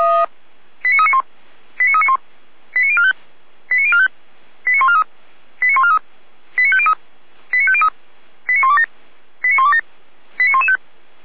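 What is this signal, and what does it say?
Five-tone selective call (ZVEI), heard as quick runs of five tones, tones of about 70 ms. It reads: 89391 89391 89045 89045 89143 89143 89503 89503 89127 89127 89115
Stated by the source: ZVEI five-tone selective call for 89126, received by SDR.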